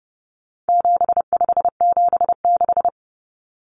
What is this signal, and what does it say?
Morse code, heard as a keyed tone, about 30 wpm, 700 Hz. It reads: 7576
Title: Common Cut Numbers - 4-Digit 30wpm